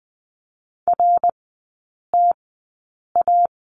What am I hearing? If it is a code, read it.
Morse code, heard as a keyed tone, about 20 wpm, spaced as for 8 wpm, 700 Hz. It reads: RTA
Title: 067 Lesson 12e - Letter d - Three Characters 20wpm